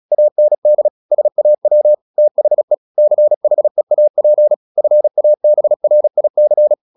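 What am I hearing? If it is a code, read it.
Morse code, heard as a keyed tone, about 36 wpm, 600 Hz. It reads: AND SAW THE CHEAP FABRIC
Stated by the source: Morse practice at 36 wpm, text of Ken Follett's novel Never, from CW med Gnister.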